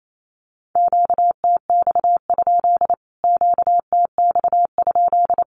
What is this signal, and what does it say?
Morse code, heard as a keyed tone, about 28 wpm, 700 Hz. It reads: QTX? QTX?
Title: Q-Codes Repeated 3x 28wpm